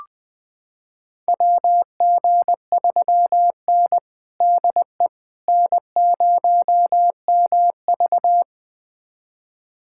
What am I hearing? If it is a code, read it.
Morse code, heard as a keyed tone, about 20 wpm, 700 Hz. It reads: WG3N DE N0MV
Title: USA Callsign DE Callsign - 4 Characters - 20wpm